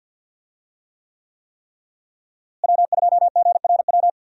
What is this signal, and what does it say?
Morse code, heard as a keyed tone, about 46 wpm, 700 Hz. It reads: W2GRW
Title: USA Call Signs - 5 Characters - Repeated 3x - 50wpm